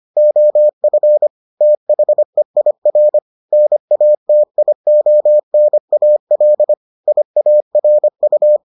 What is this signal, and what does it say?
Morse code, heard as a keyed tone, about 25 wpm, 600 Hz. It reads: OF THEIR NATIONAL IARU